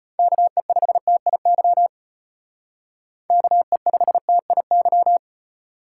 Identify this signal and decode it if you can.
Morse code, heard as a keyed tone, about 33 wpm, 700 Hz. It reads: KE5TIY KE5TIY